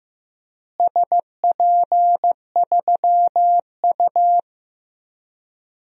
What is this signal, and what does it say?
Morse code, heard as a keyed tone, about 15 wpm, 700 Hz. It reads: SP3U
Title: DXCC-100 Call Signs - 4 Characters - Repeated 2x - 15wpm